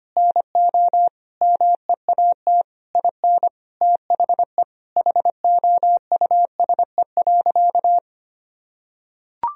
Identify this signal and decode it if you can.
Morse code, heard as a keyed tone, about 25 wpm, 700 Hz. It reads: NO MEAT IN THE HOUSE.
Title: Sentences from Top 1000 Words - Review Part 3 - 25wpm